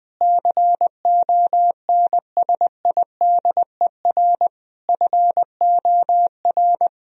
Morse code, heard as a keyed tone, about 20 wpm, 700 Hz. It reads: CONSIDER FOR
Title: Sentences from Top 1000 Words - Review Part 4 - 20wpm